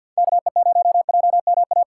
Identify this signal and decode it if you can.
Morse code, heard as a keyed tone, about 53 wpm, 700 Hz. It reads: KE0JMA
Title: USA Call Signs - 3x Character Spacing - 50wpm